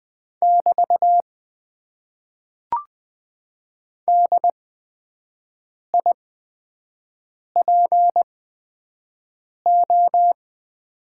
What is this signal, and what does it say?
Morse code, heard as a keyed tone, about 20 wpm, spaced as for 5 wpm, 700 Hz. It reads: <BT> DIPO
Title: Top 90 QSO Elements - 8x Character Spacing - 20wpm